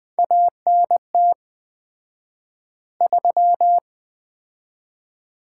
Morse code, heard as a keyed tone, about 20 wpm, 700 Hz. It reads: ANT 3
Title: Common QSO Phrases - 4x Word-Spacing - 20wpm